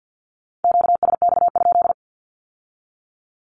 Morse code, heard as a keyed tone, about 50 wpm, 700 Hz. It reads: QSX?
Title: Q-Codes with Spoken Meaning - 50wpm